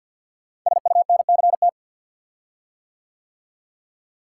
Morse code, heard as a keyed tone, about 50 wpm, 700 Hz. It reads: SUNCT
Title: Code Groups - 5 Letter:Numbers - 50wpm